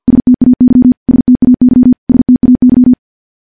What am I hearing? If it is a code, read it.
Morse code, heard as a keyed tone, about 50 wpm, 250 Hz. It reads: STAY STAY STAY
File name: morse / morse-training-50WPM